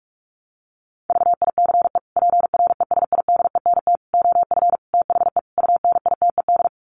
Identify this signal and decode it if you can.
Morse code, heard as a keyed tone, about 44 wpm, 700 Hz. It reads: VICE PRESIDENT OF THE UNITED
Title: Sets of 5 Words Encore2 45wpm